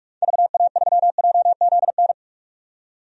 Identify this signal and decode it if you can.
Morse code, heard as a keyed tone, about 47 wpm, 700 Hz. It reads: VA3JZN
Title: DXCC-100 Call Signs - 6 Characters - 45wpm